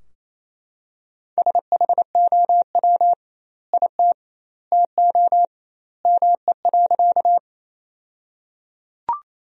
Morse code, heard as a keyed tone, about 28 wpm, 700 Hz. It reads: SHOW IT TO ME.
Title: 2x Word-Spacing - Sentences from Top 200 Words - 28wpm